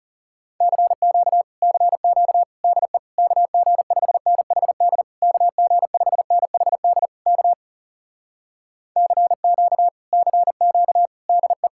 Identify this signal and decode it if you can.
Morse code, heard as a keyed tone, about 40 wpm, 700 Hz. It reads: CQ CQ DE KG5NHD KG5NHD K CQ CQ DE